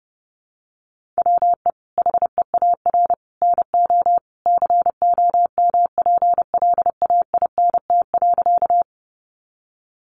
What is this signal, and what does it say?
Morse code, heard as a keyed tone, about 30 wpm, 700 Hz. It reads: WE HEAR NO COMPLAINT.
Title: Sentences from Top 1400 Words 30wpm